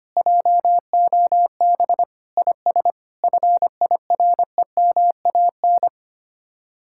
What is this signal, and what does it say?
Morse code, heard as a keyed tone, about 32 wpm, 700 Hz. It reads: JOB IS FIREMAN